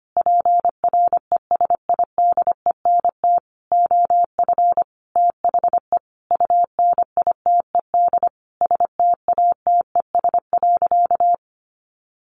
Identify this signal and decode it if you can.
Morse code, heard as a keyed tone, about 25 wpm, 700 Hz. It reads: PRESIDENT OF THE UNITED STATES.
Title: A Sound Of Thunder 25wpm